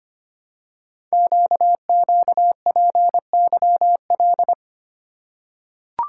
Morse code, heard as a keyed tone, about 25 wpm, 700 Hz. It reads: QQPYL
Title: Code Groups - 5 Letters - 25wpm